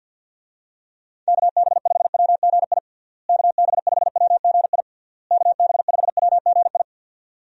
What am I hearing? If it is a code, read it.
Morse code, heard as a keyed tone, about 46 wpm, 700 Hz. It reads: KB5WGI KB5WGI KB5WGI
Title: USA Call Signs - 6 Characters - Repeated 3x - 50wpm